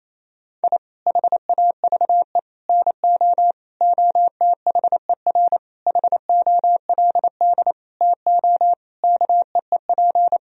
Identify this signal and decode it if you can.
Morse code, heard as a keyed tone, about 28 wpm, 700 Hz. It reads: I HAVE NO OTHER HOLD TO KEEP
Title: Sentences from Top 1000 Words - Review Part 2 - 28wpm